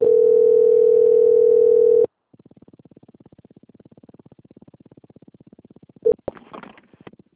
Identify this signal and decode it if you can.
Ringback tone (North America), 440 + 480 Hz.
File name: out- - - - - .9976